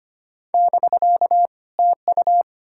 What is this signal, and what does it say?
Morse code, heard as a keyed tone, about 33 wpm, 700 Hz. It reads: <BK> TU